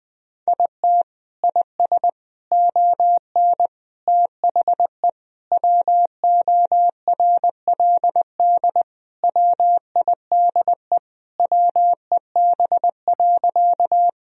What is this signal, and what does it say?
Morse code, heard as a keyed tone, about 20 wpm, 700 Hz. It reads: IT IS ON THE WORLD WIDE WEB.
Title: Sentences from Top 1000 Words - Review Part 4 - 20wpm